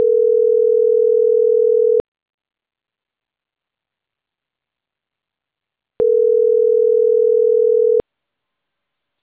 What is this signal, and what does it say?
Ringback tone (North America), 440 + 480 Hz.